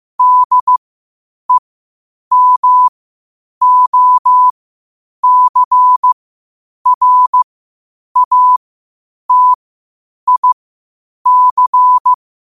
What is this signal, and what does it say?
Morse code, heard as a keyed tone, about 15 wpm, spaced as for 8 wpm, 1000 Hz. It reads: DEMOCRATIC
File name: qotdfiles / 8wpm